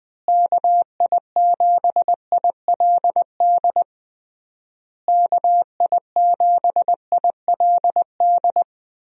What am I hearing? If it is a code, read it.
Morse code, heard as a keyed tone, about 20 wpm, 700 Hz. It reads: KI7ILD KI7ILD
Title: USA Call Signs - 6 Characters - Repeated 3x - 20wpm